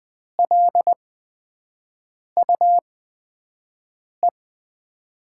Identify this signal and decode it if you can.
Morse code, heard as a keyed tone, about 20 wpm, spaced as for 5 wpm, 700 Hz. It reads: LUE
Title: USA Call Signs - 6 Characters - 8x Character Spacing - 20wpm